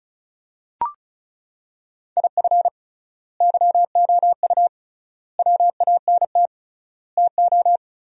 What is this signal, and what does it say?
Morse code, heard as a keyed tone, about 36 wpm, 700 Hz. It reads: IF YOU WANT TO
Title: 3x Word-Spacing - Sentences from Top 200 Words - 35wpm